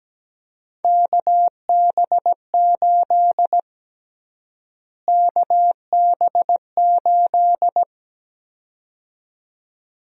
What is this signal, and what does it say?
Morse code, heard as a keyed tone, about 17 wpm, 700 Hz. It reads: KB8 KB8
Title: USA Call Sign Prefixes - Repeated 2x - 17wpm